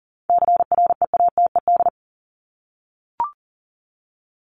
Morse code, heard as a keyed tone, about 40 wpm, 700 Hz. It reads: CREATED
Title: ED Suffix - Top 100 Words with Base Words 40wpm